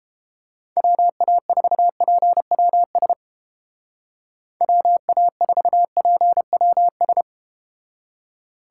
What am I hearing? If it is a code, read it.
Morse code, heard as a keyed tone, about 32 wpm, 700 Hz. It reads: WA4PWS WA4PWS